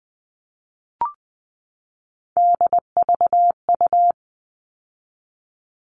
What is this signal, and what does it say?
Morse code, heard as a keyed tone, about 20 wpm, 700 Hz. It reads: DVU
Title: All Cut Numbers - 3-Digit 20wpm